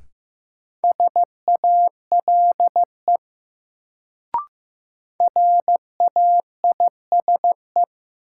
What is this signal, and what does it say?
Morse code, heard as a keyed tone, about 16 wpm, 700 Hz. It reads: SALE RAISE
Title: Sentences from Top 1400 Words - New Word Drill - 15wpm